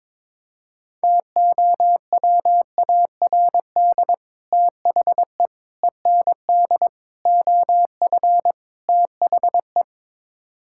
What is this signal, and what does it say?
Morse code, heard as a keyed tone, about 22 wpm, 700 Hz. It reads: TOWARD THE END OF THE
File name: practice / 22wpm / Build A Phrase - 5-Word Phrases - 22wpm